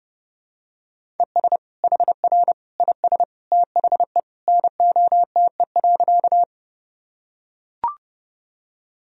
Morse code, heard as a keyed tone, about 30 wpm, 700 Hz. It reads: ES HR IS THE NOTE.
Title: Sentences from Top 500 Words and Abbreviations 30wpm